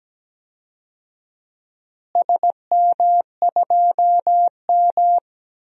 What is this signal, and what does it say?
Morse code, heard as a keyed tone, about 17 wpm, 700 Hz. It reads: SM2M
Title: DXCC-100 Call Signs - 4 Characters - Repeated 2x - 17wpm